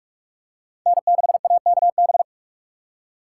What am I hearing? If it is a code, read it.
Morse code, heard as a keyed tone, about 42 wpm, 700 Hz. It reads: N6AKB